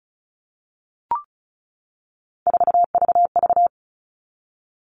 Morse code, heard as a keyed tone, about 34 wpm, 700 Hz. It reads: 4VV
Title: All Cut Numbers - 3-Digit 35wpm